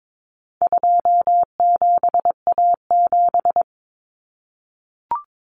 Morse code, heard as a keyed tone, about 22 wpm, 700 Hz. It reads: 27A7
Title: Common Cut Numbers - 4-Digit 22wpm